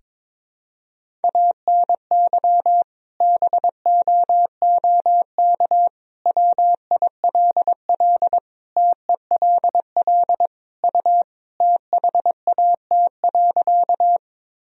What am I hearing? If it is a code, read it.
Morse code, heard as a keyed tone, about 22 wpm, 700 Hz. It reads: ANY BOOK WILL TELL U THAT.
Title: Sentences from Top 300 Words and Abbreviations 22wpm